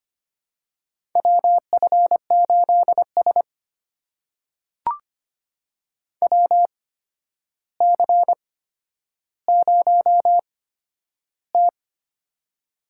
Morse code, heard as a keyed tone, about 25 wpm, spaced as for 11 wpm, 700 Hz. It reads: WF8S WC0T
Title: USA Call Signs - 4 Characters - 8x Character Spacing - 25wpm